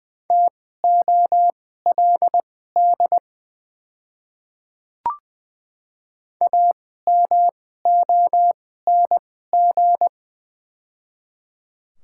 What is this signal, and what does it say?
Morse code, heard as a keyed tone, about 20 wpm, spaced as for 14 wpm, 690 Hz. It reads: TOLD AMONG
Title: 2x Character Spacing - Top 500 Words - 20wpm